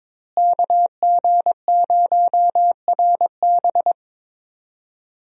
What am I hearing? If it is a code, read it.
Morse code, heard as a keyed tone, about 22 wpm, 700 Hz. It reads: KG0RB
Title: USA Call Signs - 5 Characters - Repeated 3x - 22wpm